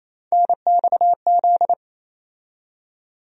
Morse code, heard as a keyed tone, about 28 wpm, 700 Hz. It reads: NXZ
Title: USA Call Sign Postfixes - 3 Letters 28wpm